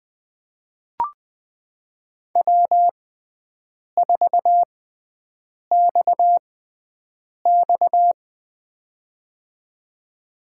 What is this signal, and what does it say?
Morse code, heard as a keyed tone, about 20 wpm, spaced as for 7 wpm, 700 Hz. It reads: W4XX
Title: USA Call Signs - 6x Character Spacing - 20wpm